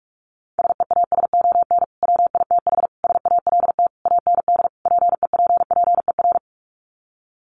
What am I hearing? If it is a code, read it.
Morse code, heard as a keyed tone, about 44 wpm, 700 Hz. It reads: SEASON WITH SALT AND PEPPER
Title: Sets of 5 Words 45wpm